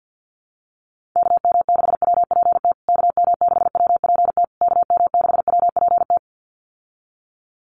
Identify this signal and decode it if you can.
Morse code, heard as a keyed tone, about 50 wpm, 700 Hz. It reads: KM6WPT KM6WPT KM6WPT
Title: USA Call Signs Repeated 3x 50wpm